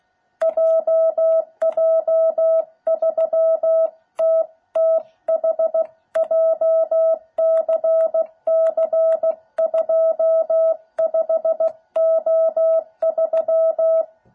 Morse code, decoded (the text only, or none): JJ3TTH JCC25O3